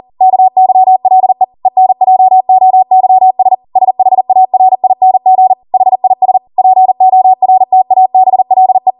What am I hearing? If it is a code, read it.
Morse code, heard as a keyed tone, about 40 wpm, 750 Hz. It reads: KYLE ENJOYS SHARING HIS PORTABLE